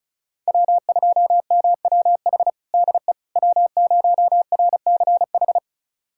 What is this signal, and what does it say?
Morse code, heard as a keyed tone, about 36 wpm, 700 Hz. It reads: W2MWH DE W0RCH
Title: USA Callsign DE Callsign - 5 Characters - 35wpm